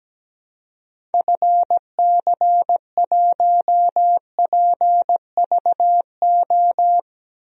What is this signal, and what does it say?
Morse code, heard as a keyed tone, about 17 wpm, 700 Hz. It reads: FC1PVO
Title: DXCC-100 Call Signs - 6 Characters - 17wpm